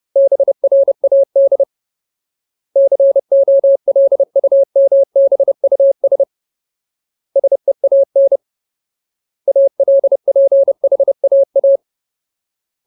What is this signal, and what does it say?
Morse code, heard as a keyed tone, about 30 wpm, 550 Hz. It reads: BRAD COLUMBUS SEAN ALPHAA
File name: Ultimate Word List / 30WPM Ultimate